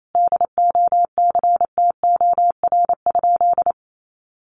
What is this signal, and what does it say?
Morse code, heard as a keyed tone, about 28 wpm, 700 Hz. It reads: DOCTOR?
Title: Sentences from Top 1500 Words 28wpm